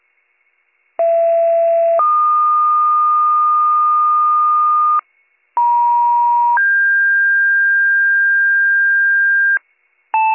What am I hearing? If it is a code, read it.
Two-tone page, heard as a one-second tone, then a three-second tone, about 670 then 1220 Hz, about 950 then 1600 Hz, with noise. It is heard from 1.0 to 9.6 s.